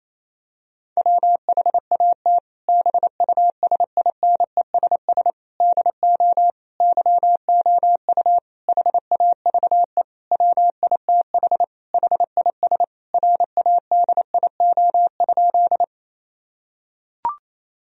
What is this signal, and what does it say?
Morse code, heard as a keyed tone, about 28 wpm, 700 Hz. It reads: WHAT BUSINESS DO YOU HAVE WITH HIS RADIO?
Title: Sentences from Top 1000 Words - Review Part 3 - 28wpm